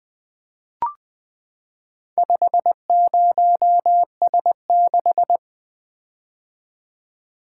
Morse code, heard as a keyed tone, about 20 wpm, 700 Hz. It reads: 50S6
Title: Four Letter-Number Combinations - 20wpm